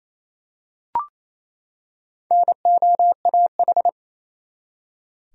Morse code, heard as a keyed tone, about 28 wpm, 700 Hz. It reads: NOAH
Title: USA Common Names - Max 4 Letters - 28wpm